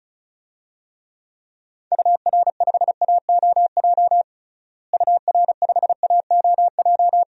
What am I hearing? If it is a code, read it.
Morse code, heard as a keyed tone, about 34 wpm, 700 Hz. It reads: UR5AOJ UR5AOJ
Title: DXCC-100 Call Signs - 6 Characters - Repeated 2x - 35wpm